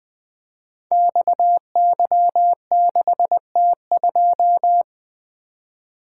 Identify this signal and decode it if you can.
Morse code, heard as a keyed tone, about 20 wpm, 700 Hz. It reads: XY6T2
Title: Code Groups - 5 Letter:Numbers - 20wpm